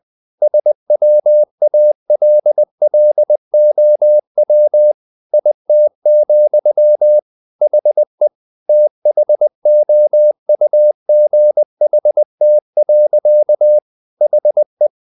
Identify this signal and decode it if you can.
Morse code, heard as a keyed tone, about 20 wpm, 600 Hz. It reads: SWALLOW IT, HE THOUGHT. HE
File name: the old man and the sea / the old man and the sea part 1 20wpm